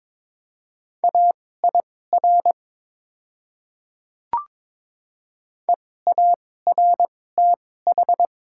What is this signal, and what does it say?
Morse code, heard as a keyed tone, about 22 wpm, spaced as for 16 wpm, 700 Hz. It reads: AIR EARTH
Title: 2x Character Spacing - Top 300 Words - 22wpm